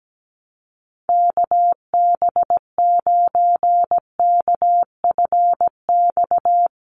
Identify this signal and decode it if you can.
Morse code, heard as a keyed tone, about 17 wpm, 700 Hz. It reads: KB9KFX
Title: USA Call Signs 17wpm